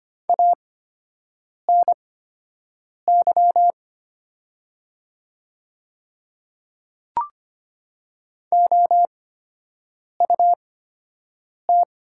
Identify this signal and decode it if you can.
Morse code, heard as a keyed tone, about 25 wpm, spaced as for 7 wpm, 700 Hz. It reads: ANY OUT
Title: 8x Character Spacing - Top 200 Words - 25wpm